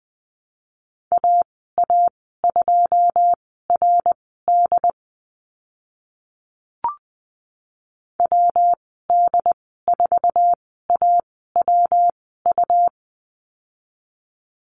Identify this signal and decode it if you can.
Morse code, heard as a keyed tone, about 20 wpm, spaced as for 14 wpm, 700 Hz. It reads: AA2RD WD4AWU
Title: 140 Lesson 23g - Letter f - Call Signs 20wpm